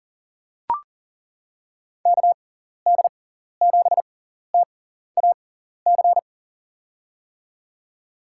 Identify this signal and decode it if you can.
Morse code, heard as a keyed tone, about 40 wpm, spaced as for 14 wpm, 700 Hz. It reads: KD7TAC
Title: USA Call Signs - 6 Characters - 6x Character Spacing - 40wpm